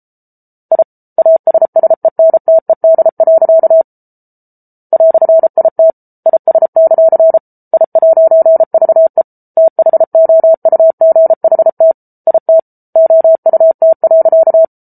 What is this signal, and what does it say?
Morse code, heard as a keyed tone, about 34 wpm, 650 Hz. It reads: I ASSENTED. "IT IS; I'VE THOUGHT IT OUT.